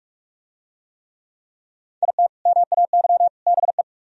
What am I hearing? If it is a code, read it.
Morse code, heard as a keyed tone, about 44 wpm, 700 Hz. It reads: IT MAY BE